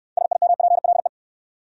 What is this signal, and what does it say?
Morse code, heard as a keyed tone, about 62 wpm, 700 Hz. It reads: SIMPLE